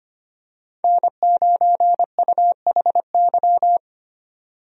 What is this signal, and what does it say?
Morse code, heard as a keyed tone, about 25 wpm, 700 Hz. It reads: N9UHY
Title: USA Call Signs - 5 Characters - 25wpm